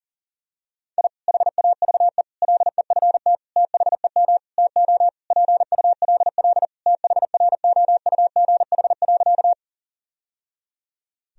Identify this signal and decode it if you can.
Morse code, heard as a keyed tone, about 40 wpm, 700 Hz. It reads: I HAVE LEFT THEM TO PULL THROUGH.
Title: Sentences from Top 1000 Words - Review Part 2 - 40wpm